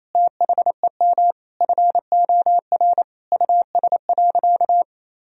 Morse code, heard as a keyed tone, about 28 wpm, 700 Hz. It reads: THEM FOR US.
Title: Sentences from Top 1000 Words - Review Part 4 - 28wpm